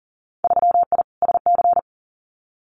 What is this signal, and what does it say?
Morse code, heard as a keyed tone, about 42 wpm, 700 Hz. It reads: 3I SC
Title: Winter Field Day Exchanges Encore 40wpm